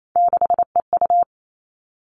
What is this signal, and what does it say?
Morse code, heard as a keyed tone, about 28 wpm, 700 Hz. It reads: 6EU
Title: All Cut Numbers - 3-Digit 28wpm